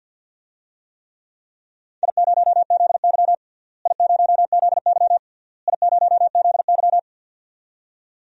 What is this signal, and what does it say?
Morse code, heard as a keyed tone, about 46 wpm, 700 Hz. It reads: I0ZY I0ZY I0ZY
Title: DXCC-100 Call Signs - 4 Characters - Repeated 3x - 50wpm